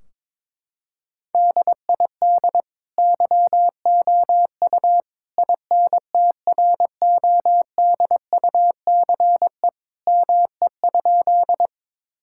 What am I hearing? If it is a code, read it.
Morse code, heard as a keyed tone, about 22 wpm, 700 Hz. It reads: DID YOU INTRODUCE ME?
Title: Sentences from Top 1600 Words 22wpm